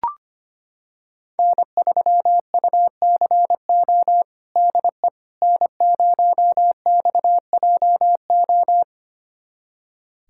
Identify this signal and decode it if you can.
Morse code, heard as a keyed tone, about 25 wpm, 700 Hz. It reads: N3UCO DE N0XJO
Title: USA Callsign DE Callsign - 5 Characters - 25wpm